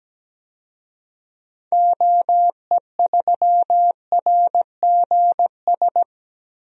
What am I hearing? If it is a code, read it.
Morse code, heard as a keyed tone, about 17 wpm, 700 Hz. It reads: OE3RGS
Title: DXCC-100 Call Signs - 6 Characters - 17wpm